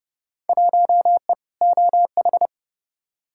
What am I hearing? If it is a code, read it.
Morse code, heard as a keyed tone, about 30 wpm, 700 Hz. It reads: 1E OH
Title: Field Day Exchanges - 30wpm